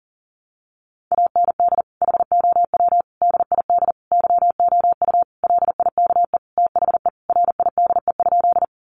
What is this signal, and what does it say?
Morse code, heard as a keyed tone, about 40 wpm, 700 Hz. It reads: AND HOW DID YOU LIKE THE RIDE?
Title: Sentences from Top 1200 Words 40wpm